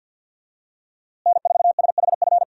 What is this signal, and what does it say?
Morse code, heard as a keyed tone, about 50 wpm, 700 Hz. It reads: N4SHF
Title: USA Call Signs - 5 Characters - Repeated 2x - 50wpm